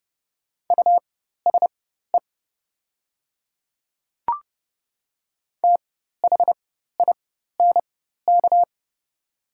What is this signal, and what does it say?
Morse code, heard as a keyed tone, about 31 wpm, spaced as for 14 wpm, 700 Hz. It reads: USE THINK
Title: 4x Character Spacing - Top 200 Words - 30wpm